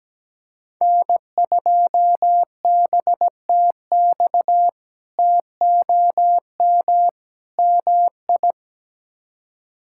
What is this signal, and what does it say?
Morse code, heard as a keyed tone, about 17 wpm, 700 Hz. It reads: N2BTX TOM MI
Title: K1USN SST QSO Practice - 17wpm